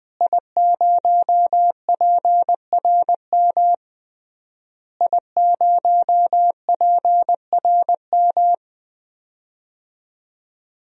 Morse code, heard as a keyed tone, about 20 wpm, 700 Hz. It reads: I0PRM I0PRM
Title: DXCC-100 Call Signs - 5 Characters - Repeated 3x - 20wpm